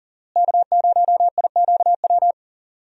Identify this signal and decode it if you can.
Morse code, heard as a keyed tone, about 40 wpm, 700 Hz. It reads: K0IQW
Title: USA Call Signs - 5 Characters - Repeated 2x - 40wpm